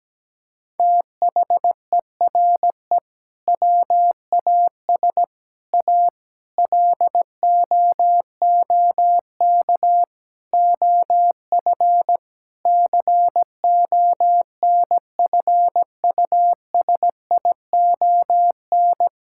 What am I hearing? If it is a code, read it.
Morse code, heard as a keyed tone, about 17 wpm, 700 Hz. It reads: THERE WAS A LOOK OF CONFUSION
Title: Sentences from Top 1600 Words 17wpm